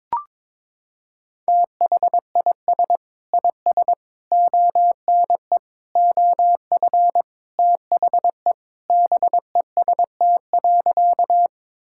Morse code, heard as a keyed tone, about 22 wpm, 700 Hz. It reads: THIS IS ONE OF THE BEST.
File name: practice / 22wpm / Sentences with 4-Word Phrases - 22wpm